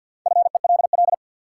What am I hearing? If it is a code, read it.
Morse code, heard as a keyed tone, about 53 wpm, 700 Hz. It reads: FELL
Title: Sentences from Top 1600 Words - New Word Drill Alt - 50wpm